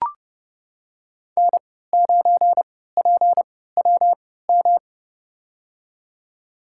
Morse code, heard as a keyed tone, about 30 wpm, spaced as for 17 wpm, 700 Hz. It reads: N9PWM
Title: USA Call Signs - 5 Characters - 3x Character Spacing - 30wpm